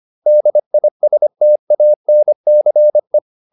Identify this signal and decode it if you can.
Morse code, heard as a keyed tone, about 25 wpm, 600 Hz. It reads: DISTANCE
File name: DIS8 25